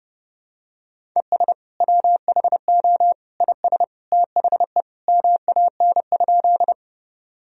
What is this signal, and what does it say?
Morse code, heard as a keyed tone, about 32 wpm, 700 Hz. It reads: ES WHO IS THE MAN?
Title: Sentences from Top 200 Words and Abbreviations 30wpm